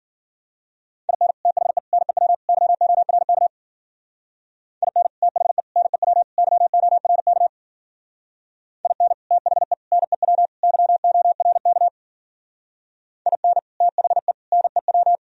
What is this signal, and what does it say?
Morse code, heard as a keyed tone, about 52 wpm, 700 Hz. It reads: IN THE NEW YORK IN THE NEW YORK IN THE NEW YORK IN THE NEW